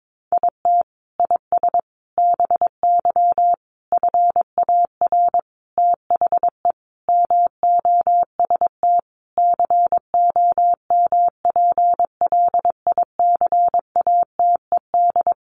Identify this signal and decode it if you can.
Morse code, heard as a keyed tone, about 22 wpm, 700 Hz. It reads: IT IS BY FAR THE MOST COMPLICATED
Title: Sentences from Top 1200 Words 22wpm